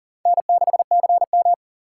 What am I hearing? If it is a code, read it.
Morse code, heard as a keyed tone, about 50 wpm, 700 Hz. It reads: N6CM
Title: Speed Racing - USA 1x2 Call Signs 40wpm